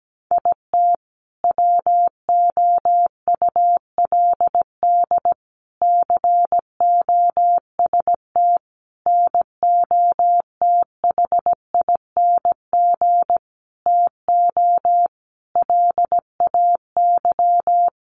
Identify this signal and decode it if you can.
Morse code, heard as a keyed tone, about 17 wpm, 700 Hz. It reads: IT WOULD COST NOTHING TO LAY
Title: Sentences from Top 800 Words 17wpm